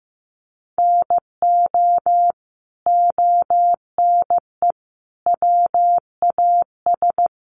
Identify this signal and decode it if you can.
Morse code, heard as a keyed tone, about 15 wpm, 700 Hz. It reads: NO ONE WAS
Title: Sentences from Top 1200 Words 15wpm